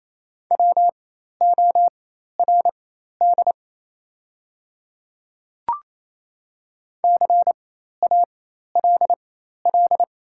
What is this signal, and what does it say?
Morse code, heard as a keyed tone, about 28 wpm, spaced as for 13 wpm, 700 Hz. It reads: WORD CALL
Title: 4x Character Spacing - Top 100 Words - 28wpm